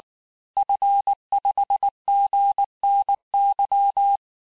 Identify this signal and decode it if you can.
Morse code, heard as a keyed tone, about 19 wpm, 800 Hz. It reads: F5GNY
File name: cours-du-lundi files / cours-1595--FIN-de-QSO-vitesse-18mpm